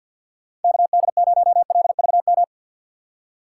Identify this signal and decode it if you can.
Morse code, heard as a keyed tone, about 52 wpm, 700 Hz. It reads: KD0LVM
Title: USA Call Signs - 6 Characters - 3x Character Spacing - 50wpm